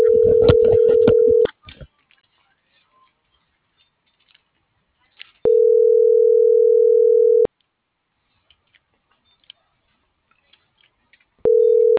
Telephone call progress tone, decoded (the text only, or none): ringback tone